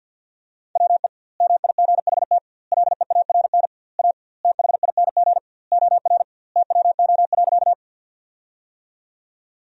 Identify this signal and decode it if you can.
Morse code, heard as a keyed tone, about 50 wpm, 700 Hz. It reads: WE MIGHT LEARN A THING OR TWO.
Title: 2x Word-Spacing - Sentences from Top 300 Words - 50wpm